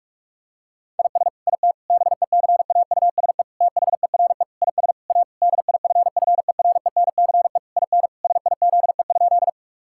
Morse code, heard as a keyed tone, about 46 wpm, 700 Hz. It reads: IS IT BECAUSE THERE IS A DIFFERENCE IN SIZE?
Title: Sentences from Top 1000 Words - Review Part 5 - 45wpm